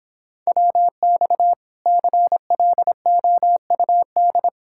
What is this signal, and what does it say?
Morse code, heard as a keyed tone, about 26 wpm, 700 Hz. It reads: WX CLOUD